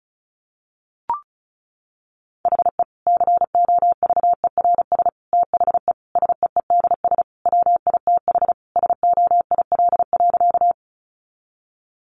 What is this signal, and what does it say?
Morse code, heard as a keyed tone, about 34 wpm, 700 Hz. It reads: HE COVERS THE SEEDS WITH SOIL.
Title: Easy Phrases 35wpm